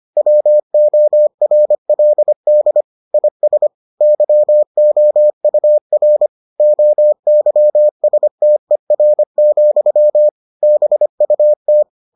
Morse code, heard as a keyed tone, about 25 wpm, 600 Hz. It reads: WORLD IS YOUR OYSTER, BUT